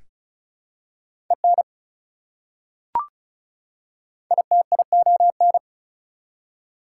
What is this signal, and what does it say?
Morse code, heard as a keyed tone, about 36 wpm, 700 Hz. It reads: EN ITION
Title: Prefixes and Suffixes 35wpm